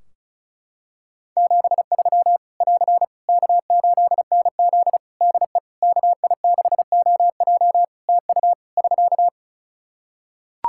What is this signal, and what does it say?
Morse code, heard as a keyed tone, about 34 wpm, 700 Hz. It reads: 73 <AR> K8NZ DE KI6OJ TU <SK>
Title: Practice QSO - Exchange 4 - Ending - 35wpm